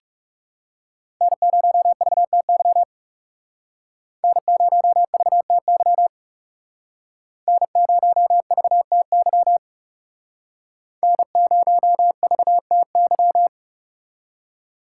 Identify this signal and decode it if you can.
Morse code, heard as a keyed tone, about 36 wpm, 700 Hz. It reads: N0VTY N0VTY N0VTY N0VTY
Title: Speed Racing - USA Call Signs 30wpm